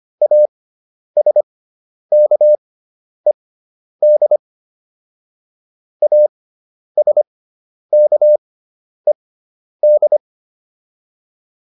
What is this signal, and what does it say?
Morse code, heard as a keyed tone, about 25 wpm, spaced as for 10 wpm, 600 Hz. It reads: ASKED ASKED
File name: ED5 10